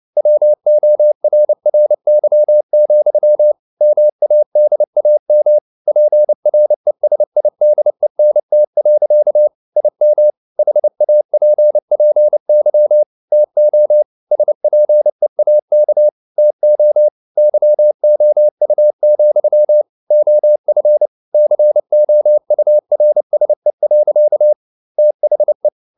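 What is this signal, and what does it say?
Morse code, heard as a keyed tone, about 29 wpm, 600 Hz. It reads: WORRY, MADAM PRESIDENT. IM HAPPY TO SPEAK TO YOU, OF COURSE. THE
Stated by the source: Morse code practice at 28 wpm, text of Ken Follett's novel Never in English, from CW med Gnister.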